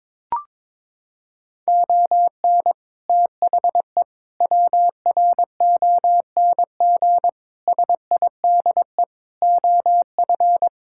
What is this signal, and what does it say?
Morse code, heard as a keyed tone, about 22 wpm, 700 Hz. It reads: ON THE WRONG SIDE OF
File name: practice / 22wpm / Sets of 6 Words - 22wpm